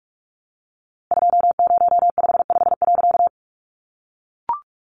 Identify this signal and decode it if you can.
Morse code, heard as a keyed tone, about 45 wpm, 700 Hz. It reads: 2055.
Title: A Sound Of Thunder 45wpm